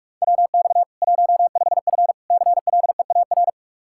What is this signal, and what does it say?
Morse code, heard as a keyed tone, about 41 wpm, 700 Hz. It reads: WX 15F CLEAR
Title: Practice QSO - Exchange 2 - Info - 45wpm